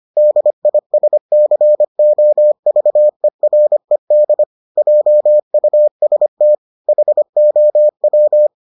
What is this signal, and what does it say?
Morse code, heard as a keyed tone, about 25 wpm, 600 Hz. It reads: DISCOVERED JUST HOW